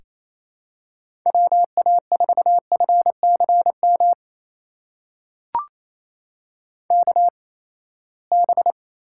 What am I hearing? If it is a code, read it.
Morse code, heard as a keyed tone, about 27 wpm, spaced as for 19 wpm, 700 Hz. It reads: WA4FCM KB
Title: USA Call Signs - 6 Characters - 8x Character Spacing - 28wpm